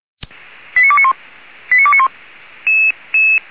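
Five-tone selective call (ZVEI), heard as quick runs of five tones, tones of about 70 ms. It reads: 89291 89291
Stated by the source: ZVEI five-tone selective call 89291 received on an SDR.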